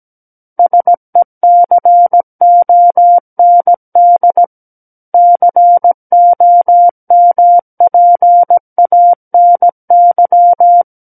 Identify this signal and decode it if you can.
Morse code, heard as a keyed tone, about 17 wpm, 700 Hz. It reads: SECOND COMPANY
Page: SKCC Learning Center - Morse Code Practice Files